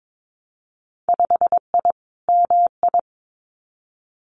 Morse code, heard as a keyed tone, about 22 wpm, 700 Hz. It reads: 5I MI
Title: Winter Field Day Exchanges Encore 22wpm